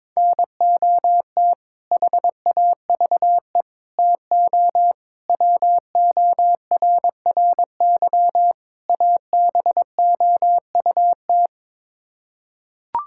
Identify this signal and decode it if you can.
Morse code, heard as a keyed tone, about 21 wpm, 700 Hz. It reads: NOT HAVE TO WORRY ABOUT
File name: practice / 22wpm / Build A Phrase - 5-Word Phrases - 22wpm